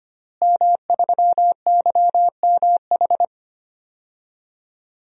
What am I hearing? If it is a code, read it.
Morse code, heard as a keyed tone, about 25 wpm, 700 Hz. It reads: M3YMH
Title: DXCC-100 Call Signs - 5 Characters - Repeated 2x - 25wpm